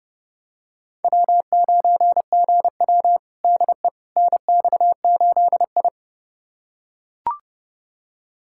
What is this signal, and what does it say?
Morse code, heard as a keyed tone, about 29 wpm, 700 Hz. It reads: W9GW DE NX8I
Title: USA Callsign DE Callsign - 4 Characters - 30wpm